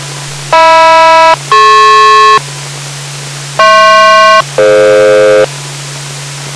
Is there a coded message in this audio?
Yes: aircraft SELCAL from 0.5 to 5.4 s, heard as two pairs of tones. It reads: HM-DP HP-DF